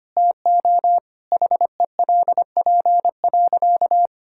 Morse code, heard as a keyed tone, about 25 wpm, 700 Hz. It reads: TO HELP.
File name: practice / 25wpm / Sentences with 4-Word Phrases - 25wpm